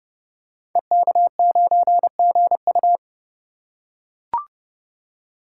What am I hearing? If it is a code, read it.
Morse code, heard as a keyed tone, about 30 wpm, 700 Hz. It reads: EK9GU
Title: Code Groups - 5 Letter:Numbers - 30wpm